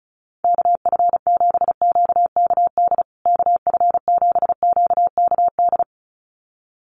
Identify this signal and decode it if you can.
Morse code, heard as a keyed tone, about 34 wpm, 700 Hz. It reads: KF7QKD KF7QKD
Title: USA Call Signs Repeated 2x 35wpm